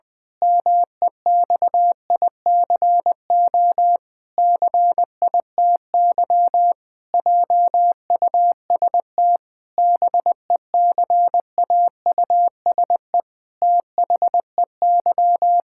Morse code, heard as a keyed tone, about 20 wpm, 700 Hz. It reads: MEXICO CITY JUST BECAUSE THEY
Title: Pangram Challenge v2 - 20wpm